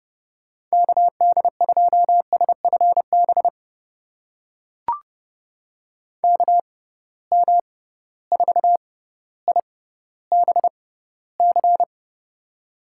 Morse code, heard as a keyed tone, about 30 wpm, spaced as for 15 wpm, 700 Hz. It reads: KD2SFB KM4IBC
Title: USA Call Signs - 6 Characters - 6x Character Spacing - 30wpm